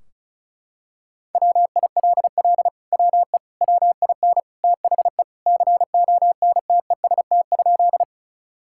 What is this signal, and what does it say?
Morse code, heard as a keyed tone, about 34 wpm, 700 Hz. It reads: WILL WE WIN THE CONTEST?
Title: Sentences from Top 1600 Words 35wpm